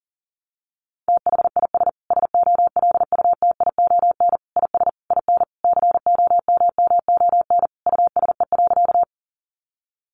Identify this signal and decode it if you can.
Morse code, heard as a keyed tone, about 40 wpm, 700 Hz. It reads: THIS SOLUTION IS IN COMMON USE.
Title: Sentences from Top 1300 Words 40wpm